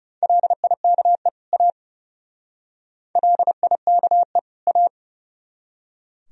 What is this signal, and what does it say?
Morse code, heard as a keyed tone, about 38 wpm, 700 Hz. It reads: LIKE A LIKE A
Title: Speed-Racing - Sets of 2 Words 30wpm